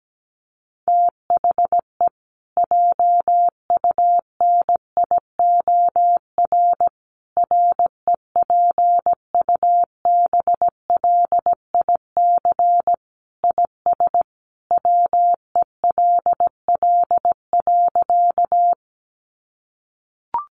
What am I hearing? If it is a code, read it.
Morse code, heard as a keyed tone, about 17 wpm, 700 Hz. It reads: THE JUNIOR REPUBLIC IS WELL.
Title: Sentences from Top 1500 Words 17wpm